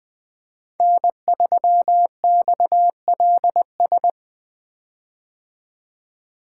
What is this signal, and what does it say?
Morse code, heard as a keyed tone, about 21 wpm, 700 Hz. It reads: N3XLS
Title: 2023 POTA Hunter Call Signs - 20wpm